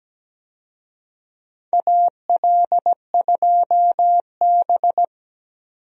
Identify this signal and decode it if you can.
Morse code, heard as a keyed tone, about 17 wpm, 700 Hz. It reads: AL2B
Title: USA Call Signs - 4 Characters - 17wpm